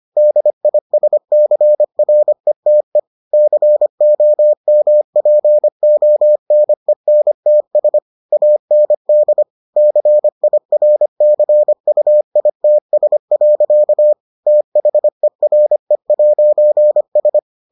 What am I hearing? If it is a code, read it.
Morse code, heard as a keyed tone, about 25 wpm, 600 Hz. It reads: DISCRETE COMPONENTS AND CIRCUITS. THERE'S